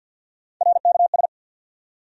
Morse code, heard as a keyed tone, about 46 wpm, 700 Hz. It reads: RKS
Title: USA Call Sign Postfixes - 3 Letters 50wpm